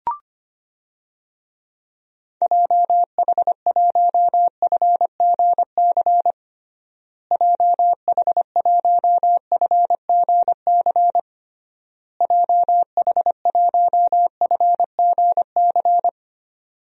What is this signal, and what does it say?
Morse code, heard as a keyed tone, about 25 wpm, 700 Hz. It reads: JH1FGC JH1FGC JH1FGC
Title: DXCC-100 Call Signs - 6 Characters - Repeated 3x - 25wpm